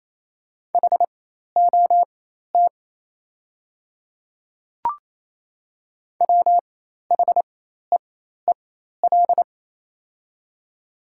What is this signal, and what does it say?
Morse code, heard as a keyed tone, about 28 wpm, spaced as for 13 wpm, 700 Hz. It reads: HOT WHEEL
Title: 4x Character Spacing - Top 500 Words - 28wpm